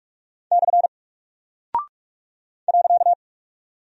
Morse code, heard as a keyed tone, about 44 wpm, 700 Hz. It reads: / .
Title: Single Character (Letter:Number:Punctuation:Prosign) Alt 45wpm